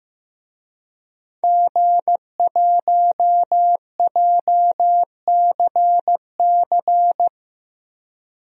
Morse code, heard as a keyed tone, about 16 wpm, 700 Hz. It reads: G1JCC
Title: DXCC-100 Call Signs - 5 Characters - 15wpm